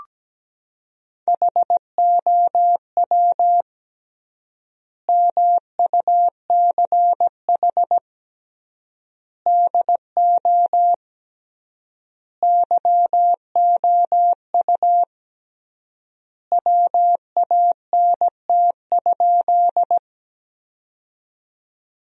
Morse code, heard as a keyed tone, about 17 wpm, 700 Hz. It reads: HOW MUCH DO YOU WANT?
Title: 3x Word-Spacing - Sentences from Top 200 Words - 17wpm